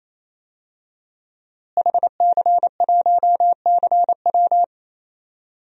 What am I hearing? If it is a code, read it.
Morse code, heard as a keyed tone, about 28 wpm, 700 Hz. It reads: HC1CW
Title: DXCC-100 Call Signs - 5 Characters - Repeated 3x - 28wpm